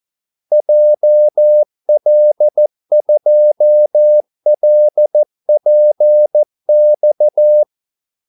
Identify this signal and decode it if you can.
Morse code, heard as a keyed tone, about 14 wpm, 600 Hz. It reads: JL2LPX
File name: Callsigns-14wpm